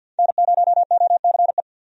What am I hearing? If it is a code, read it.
Morse code, heard as a keyed tone, about 47 wpm, 700 Hz. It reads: N0OCE
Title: USA Call Signs - 5 Characters - 50wpm